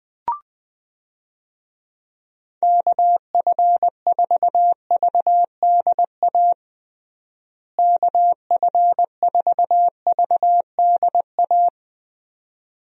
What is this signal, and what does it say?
Morse code, heard as a keyed tone, about 19 wpm, 700 Hz. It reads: KF4VDA KF4VDA
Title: USA Call Signs - 6 Characters - Repeated 2x - 20wpm